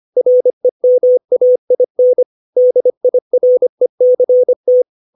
Morse code, heard as a keyed tone, about 25 wpm, 500 Hz. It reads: REMAIN DIRECT
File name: 632 words / 632 Words 25WPM